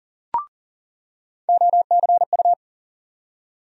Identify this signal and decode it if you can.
Morse code, heard as a keyed tone, about 40 wpm, 700 Hz. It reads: OCU
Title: USA Call Sign Postfixes - 3 Letters 40wpm